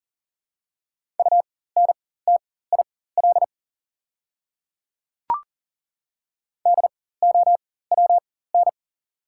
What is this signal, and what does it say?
Morse code, heard as a keyed tone, about 39 wpm, spaced as for 18 wpm, 690 Hz. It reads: UNTIL DOWN
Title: 4x Character Spacing - Top 500 Words - 40wpm